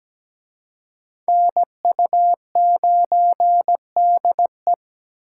Morse code, heard as a keyed tone, about 20 wpm, 700 Hz. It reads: NU9DE
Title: Speed Racing - USA 2x2 Call Signs 17wpm